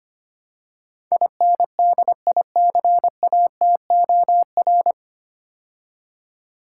Morse code, heard as a keyed tone, about 25 wpm, 700 Hz. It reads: INDICATOR
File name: practice / 25wpm / 9 Letter Words - First 3 Letters Given - 25wpm